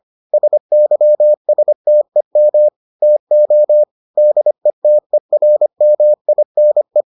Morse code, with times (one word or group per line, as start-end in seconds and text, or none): 0.3-2.7 SYSTEM
3.0-3.8 TO
4.2-7.0 DETERMINE